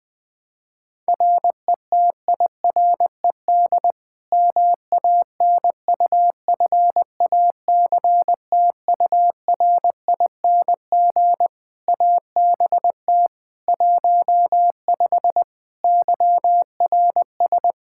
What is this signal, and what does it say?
Morse code, heard as a keyed tone, about 20 wpm, 700 Hz. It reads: RETIRED MANUFACTURING ABT 15 YRS
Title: Practice QSO - Exchange 3 - More Info - 20wpm